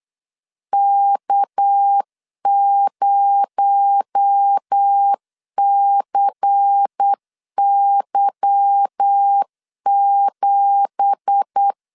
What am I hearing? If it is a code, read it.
Morse code, heard as a keyed tone, about 8 wpm, 800 Hz. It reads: K0CY7
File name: 8wpm